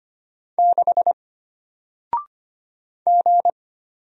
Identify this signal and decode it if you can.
Morse code, heard as a keyed tone, about 25 wpm, 700 Hz. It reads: 6 G
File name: Single-Letter-Number-25wpm